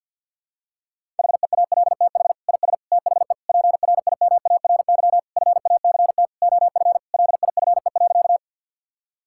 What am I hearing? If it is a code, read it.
Morse code, heard as a keyed tone, about 48 wpm, 700 Hz. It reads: HEALTH IS THE PRIMARY FACT OF LIFE.
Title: Sentences from Top 1000 Words - Review Part 5 - 50wpm